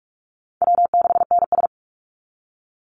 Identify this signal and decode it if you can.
Morse code, heard as a keyed tone, about 44 wpm, 700 Hz. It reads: W6NS
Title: Call Signs Repeated 3x - 45wpm